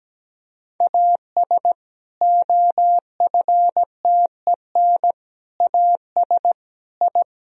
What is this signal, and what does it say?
Morse code, heard as a keyed tone, about 17 wpm, 700 Hz. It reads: AS OFTEN AS I
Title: Sentences from Top 1000 Words - Review Part 1 - 17wpm